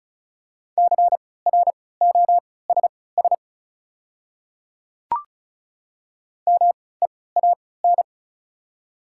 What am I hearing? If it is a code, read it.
Morse code, heard as a keyed tone, about 36 wpm, spaced as for 20 wpm, 700 Hz. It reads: CROSS MEAN
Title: 3x Character Spacing - Top 300 Words - 35wpm